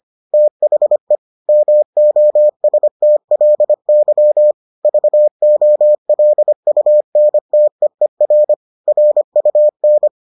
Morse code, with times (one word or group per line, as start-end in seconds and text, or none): 0.3-1.2 THE
1.5-4.5 MOSTLY
4.8-8.5 VOLUNTEER
8.9-10.1 RUN